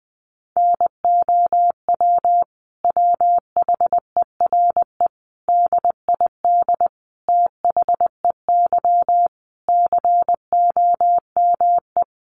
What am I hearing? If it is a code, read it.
Morse code, heard as a keyed tone, about 20 wpm, 700 Hz. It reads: NOW WHERE DID THEY COME
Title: Sentences from Top 200 Words 20wpm